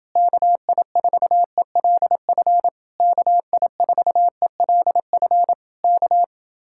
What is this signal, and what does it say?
Morse code, heard as a keyed tone, about 31 wpm, 700 Hz. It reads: KI4ELF KI4ELF K